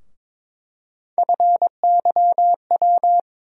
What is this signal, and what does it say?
Morse code, heard as a keyed tone, about 22 wpm, 700 Hz. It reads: FYW
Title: USA Call Sign Postfixes - 3 Letters 22wpm